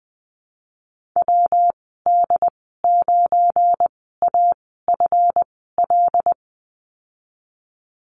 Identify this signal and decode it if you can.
Morse code, heard as a keyed tone, about 19 wpm, spaced as for 14 wpm, 700 Hz. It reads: WD9AFL
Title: 179 Lesson 29g - Slash - Call Signs 20wpm